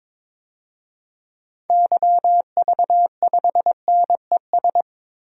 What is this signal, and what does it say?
Morse code, heard as a keyed tone, about 22 wpm, 700 Hz. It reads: YV5NES
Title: DXCC-100 Call Signs - 6 Characters - Repeated 2x - 22wpm